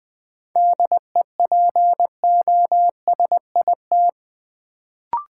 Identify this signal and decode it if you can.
Morse code, heard as a keyed tone, about 20 wpm, 700 Hz. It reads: DEPOSIT
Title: Sentences from Top 1300 Words - New Word Drill - 20wpm